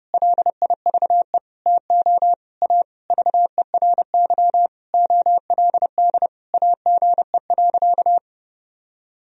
Morse code, heard as a keyed tone, about 32 wpm, 700 Hz. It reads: LIVE TO A VERY OLD AGE.
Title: Sentences from Top 1000 Words - Review Part 3 - 30wpm